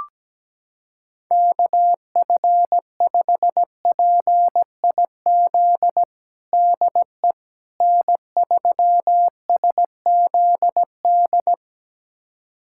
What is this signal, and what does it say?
Morse code, heard as a keyed tone, about 17 wpm, 700 Hz. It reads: KF5PIZ DE N3SZD
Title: USA Callsign DE Callsign 17wpm